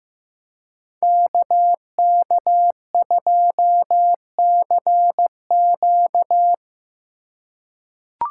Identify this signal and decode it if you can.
Morse code, heard as a keyed tone, about 15 wpm, 700 Hz. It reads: KK2CQ
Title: USA Call Signs - 5 Characters - 4x Character Spacing - 15wpm